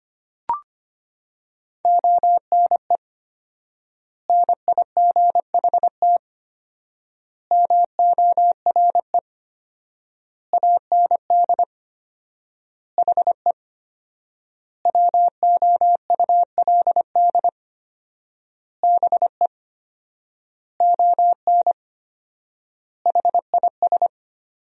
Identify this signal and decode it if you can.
Morse code, heard as a keyed tone, about 25 wpm, 700 Hz. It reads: ONE NIGHT MORE AND HE WOULD BE ON HIS
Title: 4x Word-Spacing - Sentences from Top 300 Words - 25wpm